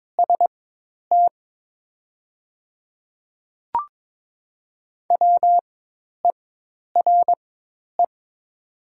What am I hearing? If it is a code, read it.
Morse code, heard as a keyed tone, about 22 wpm, spaced as for 10 wpm, 700 Hz. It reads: ST WERE
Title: 4x Character Spacing - Top 200 Words - 22wpm